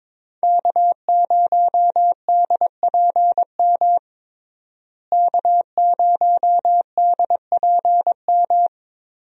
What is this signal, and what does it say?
Morse code, heard as a keyed tone, about 22 wpm, 700 Hz. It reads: K0DPM K0DPM
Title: USA Call Signs - 5 Characters - Repeated 2x - 22wpm